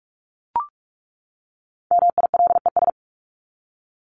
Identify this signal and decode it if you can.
Morse code, heard as a keyed tone, about 44 wpm, 700 Hz. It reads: MILES
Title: ES:S Suffix - Top 100 Words with Base Words 45wpm